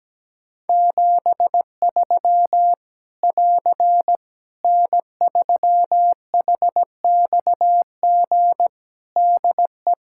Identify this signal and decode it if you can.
Morse code, heard as a keyed tone, about 17 wpm, 700 Hz. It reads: 73 <AR> N3HXG DE
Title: Practice QSO - Exchange 4 - Ending - 17wpm